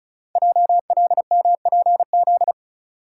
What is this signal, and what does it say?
Morse code, heard as a keyed tone, about 36 wpm, 700 Hz. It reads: JLMPZ
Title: Code Groups - 5 Letters - 35wpm